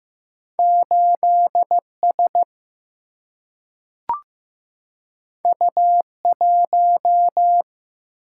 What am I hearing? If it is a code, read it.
Morse code, heard as a keyed tone, about 15 wpm, 700 Hz. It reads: 8S U1
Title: Single-Letter Single-Number 15wpm